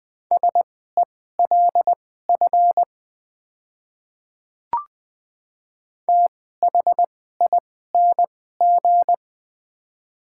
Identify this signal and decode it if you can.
Morse code, heard as a keyed tone, about 20 wpm, spaced as for 15 wpm, 700 Hz. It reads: SELF THING
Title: 2x Character Spacing - Top 200 Words - 20wpm